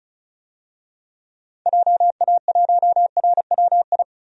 Morse code, heard as a keyed tone, about 34 wpm, 700 Hz. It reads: JA1RWI
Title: DXCC-100 Call Signs - 6 Characters - 35wpm